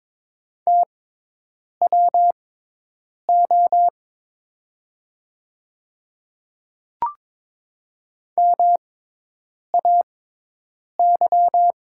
Morse code, heard as a keyed tone, about 22 wpm, spaced as for 8 wpm, 700 Hz. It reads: TWO MAY
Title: 6x Character Spacing - Top 300 Words - 22wpm